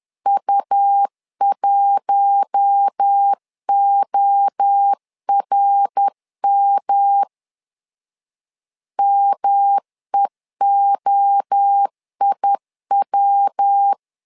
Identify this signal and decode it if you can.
Morse code, heard as a keyed tone, about 10 wpm, 800 Hz. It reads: U1ORM MEOIW
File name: LEZIONI / 4 Q1 10wpm 2448